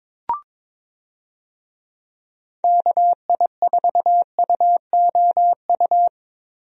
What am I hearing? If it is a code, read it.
Morse code, heard as a keyed tone, about 22 wpm, 700 Hz. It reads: KI4UOU
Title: USA Call Signs - 6 Characters - Repeated 3x - 22wpm